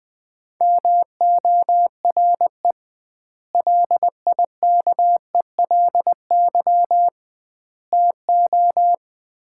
Morse code, heard as a keyed tone, about 20 wpm, 700 Hz. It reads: MORE LIKELY TO
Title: 2x Word-Spacing - Sets of 3 Words - 20wpm